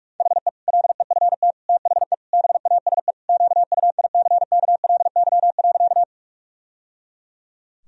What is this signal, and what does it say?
Morse code, heard as a keyed tone, about 46 wpm, 700 Hz. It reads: HE LEFT THE BASE QUICKLY.
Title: Sentences from Top 1000 Words - Review Part 1 - 45wpm